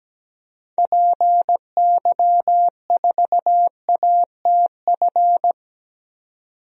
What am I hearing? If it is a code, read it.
Morse code, heard as a keyed tone, about 17 wpm, 700 Hz. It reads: PY4ATF
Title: DXCC-100 Call Signs - 6 Characters - 17wpm